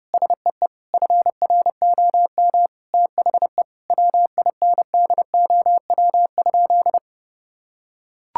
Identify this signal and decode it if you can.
Morse code, heard as a keyed tone, about 32 wpm, 700 Hz. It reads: SEE FROM THE WINDOW?
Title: Sentences from Top 1600 Words 30wpm